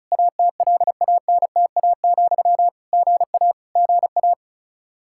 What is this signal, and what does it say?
Morse code, heard as a keyed tone, about 34 wpm, 700 Hz. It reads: ATLANTA, GA GA
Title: K1USN SST QSO Practice 35wpm